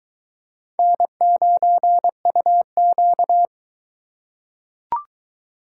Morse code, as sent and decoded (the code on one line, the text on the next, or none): -. ----. ..- --.-
N9UQ